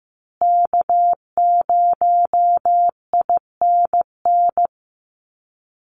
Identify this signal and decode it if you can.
Morse code, heard as a keyed tone, about 15 wpm, 700 Hz. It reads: K0INN
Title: Call Signs Repeated 2x - 15wpm